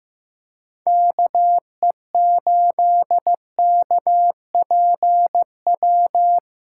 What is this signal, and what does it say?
Morse code, heard as a keyed tone, about 15 wpm, 700 Hz. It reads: KE8KPW
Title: USA Callsign DE Callsign 15wpm